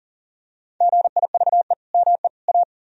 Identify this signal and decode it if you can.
Morse code, heard as a keyed tone, about 40 wpm, 700 Hz. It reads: GIVE ME A
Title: Sentences from Top 1000 Words - Review Part 2 - 40wpm